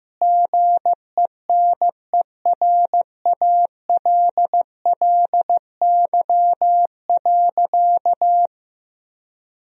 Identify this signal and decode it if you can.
Morse code, heard as a keyed tone, about 15 wpm, 700 Hz. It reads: GENERALLY.
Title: Sentences from Top 1000 Words - Review Part 4 - 15wpm